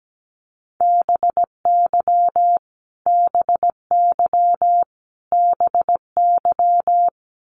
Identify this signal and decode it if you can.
Morse code, heard as a keyed tone, about 17 wpm, 700 Hz. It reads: BY BY BY
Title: Top 200 Words Repeated 17wpm